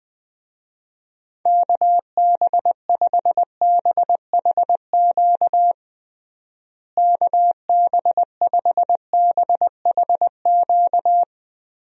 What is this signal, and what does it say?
Morse code, heard as a keyed tone, about 20 wpm, 700 Hz. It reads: KB5BHQ KB5BHQ
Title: USA Call Signs - 6 Characters - Repeated 3x - 20wpm